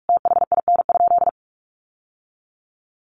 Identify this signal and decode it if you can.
Morse code, heard as a keyed tone, about 46 wpm, 700 Hz. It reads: THIN?
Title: Sentences from Top 1200 Words 45wpm